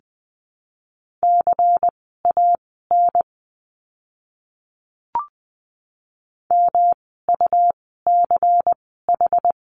Morse code, heard as a keyed tone, about 20 wpm, spaced as for 14 wpm, 700 Hz. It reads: CAN MUCH
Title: 146 Lesson 24f - Letter y - Words 20wpm